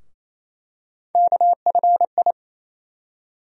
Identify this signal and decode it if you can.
Morse code, heard as a keyed tone, about 28 wpm, 700 Hz. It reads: KFI
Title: USA Call Sign Postfixes - 3 Letters 28wpm